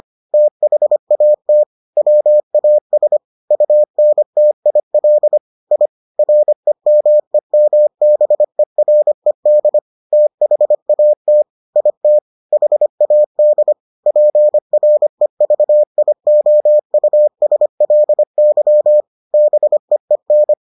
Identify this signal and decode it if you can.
Morse code, heard as a keyed tone, about 25 wpm, 600 Hz. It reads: THAT WAS UNTIL I REMEMBERED THAT IT HAD PREVIOUSLY BEEN